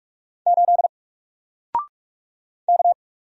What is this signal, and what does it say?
Morse code, heard as a keyed tone, about 44 wpm, 700 Hz. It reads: 8 K
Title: Single Letter-Number Alt 45wpm